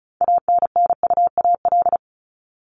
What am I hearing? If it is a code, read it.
Morse code, heard as a keyed tone, about 34 wpm, 700 Hz. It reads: ANNUAL
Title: Top 1251-1500 Words 35wpm